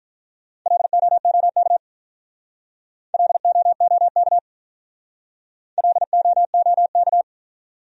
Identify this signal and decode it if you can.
Morse code, heard as a keyed tone, about 44 wpm, 700 Hz. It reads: LOOK LOOK LOOK